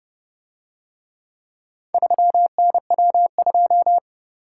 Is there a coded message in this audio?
Yes: Morse code from 1.9 to 4.0 s, heard as a keyed tone, about 29 wpm, 700 Hz. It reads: 3NW2